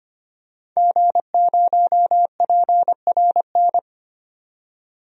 Morse code, heard as a keyed tone, about 25 wpm, 700 Hz. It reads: G0PRN
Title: DXCC-100 Call Signs - 5 Characters - 25wpm